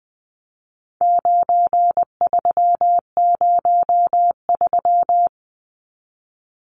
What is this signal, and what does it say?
Morse code, heard as a keyed tone, about 20 wpm, 700 Hz. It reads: 9303
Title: Numbers - 4-Digit 20wpm